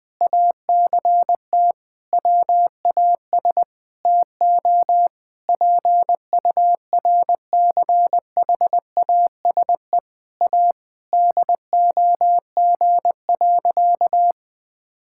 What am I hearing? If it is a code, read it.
Morse code, heard as a keyed tone, about 20 wpm, 700 Hz. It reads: ACT WAS TO PURCHASE A DOG.
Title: Sentences from Top 1000 Words - Review Part 5 - 20wpm